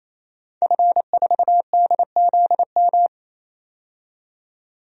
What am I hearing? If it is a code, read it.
Morse code, heard as a keyed tone, about 28 wpm, 700 Hz. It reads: F4DZM
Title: DXCC-100 Call Signs - 5 Characters - 28wpm